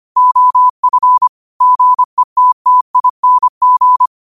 Morse code, heard as a keyed tone, about 25 wpm, 1000 Hz. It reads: OF GETTING
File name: qotdfiles / 25wpm